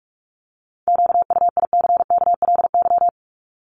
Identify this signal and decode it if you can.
Morse code, heard as a keyed tone, about 44 wpm, 700 Hz. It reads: QUICKLY
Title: Top 501-750 Words 45wpm